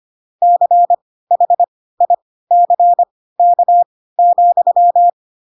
Morse code, heard as a keyed tone, about 25 wpm, spaced as for 16 wpm, 700 Hz. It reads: CHICK,